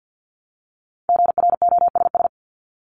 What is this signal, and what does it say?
Morse code, heard as a keyed tone, about 50 wpm, 700 Hz. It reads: GROSS
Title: Top 1751-2000 Words 50wpm